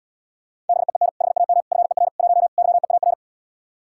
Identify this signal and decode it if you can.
Morse code, heard as a keyed tone, about 77 wpm, 700 Hz. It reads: BEEN HAM FER 1 YEAR